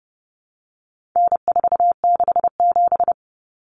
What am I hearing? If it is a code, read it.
Morse code, heard as a keyed tone, about 30 wpm, 700 Hz. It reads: N467
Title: Common Cut Numbers - 4-Digit 30wpm